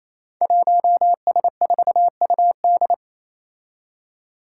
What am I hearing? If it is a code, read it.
Morse code, heard as a keyed tone, about 28 wpm, 700 Hz. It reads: 1S4UD
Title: Five Challenging Letter-Number Combinations 28wpm